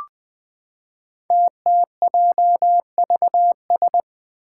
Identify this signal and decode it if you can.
Morse code, heard as a keyed tone, about 20 wpm, 700 Hz. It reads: TTJVS
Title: Five Challenging Letter-Number Combinations 20wpm